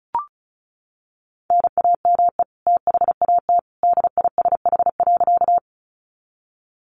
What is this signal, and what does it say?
Morse code, heard as a keyed tone, about 34 wpm, 700 Hz. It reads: NAME THAT DISH.
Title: Sentences from Top 1300 Words 35wpm